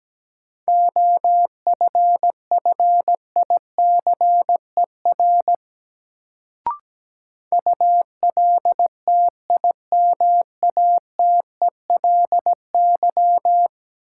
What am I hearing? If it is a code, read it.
Morse code, heard as a keyed tone, about 17 wpm, 700 Hz. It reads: OFFICER ULTIMATELY
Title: Sentences from Top 1200 Words - New Word Drill - 17wpm